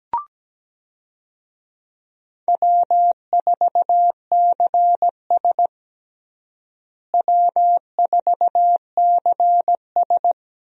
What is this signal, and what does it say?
Morse code, heard as a keyed tone, about 17 wpm, 700 Hz. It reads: W4CS W4CS
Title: USA Call Signs - 4 Characters - Repeated 2x - 17wpm